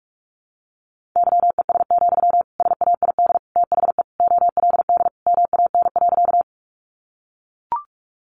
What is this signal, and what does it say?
Morse code, heard as a keyed tone, about 44 wpm, 700 Hz. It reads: YES, SAID THE OLD MAN.
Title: Sentences from Top 500 Words 45wpm